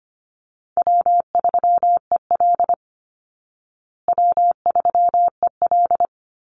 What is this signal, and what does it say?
Morse code, heard as a keyed tone, about 25 wpm, 700 Hz. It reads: W3EL W3EL
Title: Easy Call Signs Repeated 2x - 25wpm